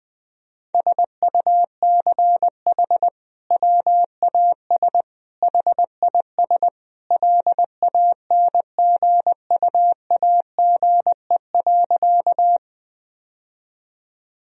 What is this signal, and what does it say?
Morse code, heard as a keyed tone, about 20 wpm, 700 Hz. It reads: SUCH WAS HIS LANGUAGE.
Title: Sentences from Top 1000 Words - Review Part 3 - 20wpm